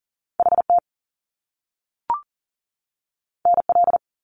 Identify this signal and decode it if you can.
Morse code, heard as a keyed tone, about 40 wpm, 700 Hz. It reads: HT NL
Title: Top Two Letter Combinations in Words 40wpm